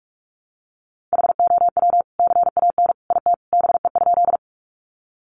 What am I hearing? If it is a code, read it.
Morse code, heard as a keyed tone, about 44 wpm, 700 Hz. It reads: HOW CAN IT BE?
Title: Sentences from Top 100 Words 45wpm